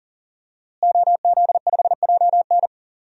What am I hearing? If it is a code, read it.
Morse code, heard as a keyed tone, about 40 wpm, 700 Hz. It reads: OZ5JN
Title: DXCC-100 Call Signs - 5 Characters - Repeated 3x - 40wpm